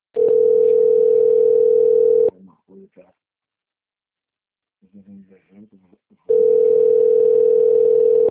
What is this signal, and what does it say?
Ringback tone (North America), 440 + 480 Hz, with speech.
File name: rg-1600-unknown- - - .13249